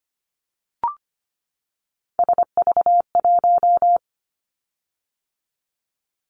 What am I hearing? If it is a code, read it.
Morse code, heard as a keyed tone, about 25 wpm, 700 Hz. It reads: SV1
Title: Callsign Prefixes for Top 50 DXCC - 25wpm